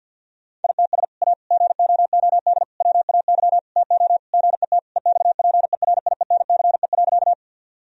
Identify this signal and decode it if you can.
Morse code, heard as a keyed tone, about 50 wpm, 700 Hz. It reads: ITS A GOOD WAY TO GET EXPERIENCE.
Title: Sentences from Top 1000 Words - Review Part 3 - 50wpm